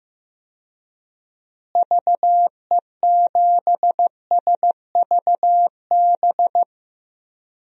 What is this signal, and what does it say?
Morse code, heard as a keyed tone, about 15 wpm, 700 Hz. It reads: VE7SVB
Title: DXCC-100 Call Signs - 6 Characters - Repeated 2x - 15wpm